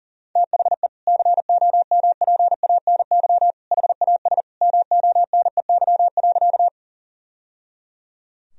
Morse code, heard as a keyed tone, about 40 wpm, 700 Hz. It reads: THE COMPANY HAS MONEY.
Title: Sentences from Top 1000 Words - Review Part 3 - 40wpm